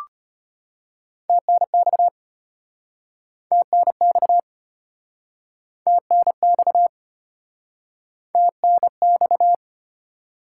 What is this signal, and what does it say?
Morse code, heard as a keyed tone, about 36 wpm, 700 Hz. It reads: TNX TNX TNX TNX